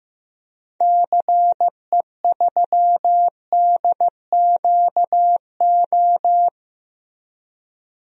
Morse code, heard as a keyed tone, about 16 wpm, 700 Hz. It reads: CE3DQO
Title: DXCC-100 Call Signs - 6 Characters - Repeated 2x - 15wpm